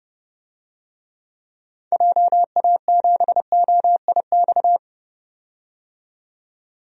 Morse code, heard as a keyed tone, about 30 wpm, 700 Hz. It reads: JA7OIX
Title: DXCC-100 Call Signs - 6 Characters - 30wpm